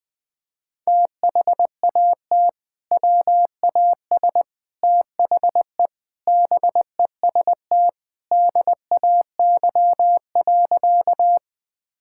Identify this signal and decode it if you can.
Morse code, heard as a keyed tone, about 20 wpm, 700 Hz. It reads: THAT WAS THE BEST DAY.
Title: Sentences with 3-Word Phrases - 20wpm